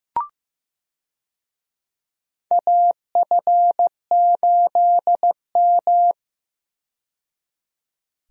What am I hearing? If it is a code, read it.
Morse code, heard as a keyed tone, about 15 wpm, 700 Hz. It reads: AF8M
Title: USA Call Signs - 4 Characters - 15wpm